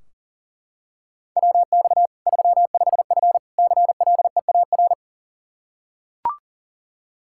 Morse code, heard as a keyed tone, about 40 wpm, 700 Hz. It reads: WX 35F CLEAR
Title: K1USN SST QSO Practice 40wpm